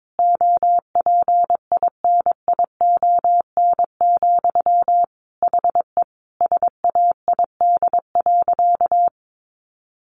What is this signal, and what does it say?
Morse code, heard as a keyed tone, about 22 wpm, 700 Hz. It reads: OPINION, HE SAID.
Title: Sentences from Top 1200 Words 22wpm